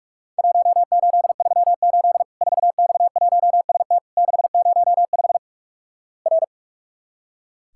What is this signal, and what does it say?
Morse code, heard as a keyed tone, about 44 wpm, 700 Hz. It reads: 1838 4X1ST 605 R
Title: Worked All Europe DX Contest - QTC Exchange - 45wpm